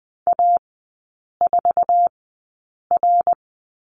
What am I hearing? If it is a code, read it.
Morse code, heard as a keyed tone, about 20 wpm, spaced as for 8 wpm, 700 Hz. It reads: A4R
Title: 067 Lesson 12e - Letter d - Three Characters 20wpm